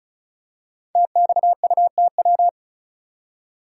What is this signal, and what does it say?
Morse code, heard as a keyed tone, about 36 wpm, 700 Hz. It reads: TXUTW
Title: Code Groups - 5 Letters - 35wpm